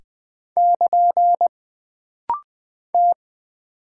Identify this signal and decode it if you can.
Morse code, heard as a keyed tone, about 20 wpm, 700 Hz. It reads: <KN> T
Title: Single Character (Letter:Number:Punctuation:Prosign) - Rapid-Fire - 20wpm